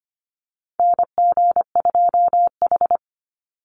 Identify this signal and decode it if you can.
Morse code, heard as a keyed tone, about 25 wpm, 700 Hz. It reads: NG2H
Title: Call Signs - 25wpm